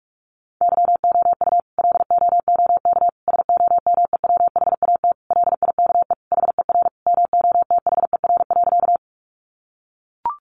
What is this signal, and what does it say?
Morse code, heard as a keyed tone, about 44 wpm, 700 Hz. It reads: YOU LOOK SOMEWHAT LIKE HER MOTHER.
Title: Sentences from Top 1300 Words 45wpm